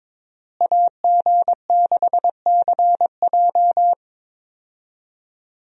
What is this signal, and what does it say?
Morse code, heard as a keyed tone, about 22 wpm, 700 Hz. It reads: AG6CJ
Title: USA Call Signs - 5 Characters - Repeated 3x - 22wpm